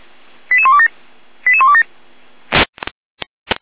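Five-tone selective call (ZVEI), heard as quick runs of five tones, tones of about 70 ms. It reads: 88127 88127
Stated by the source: ZVEI five-tone selective call received via SDR.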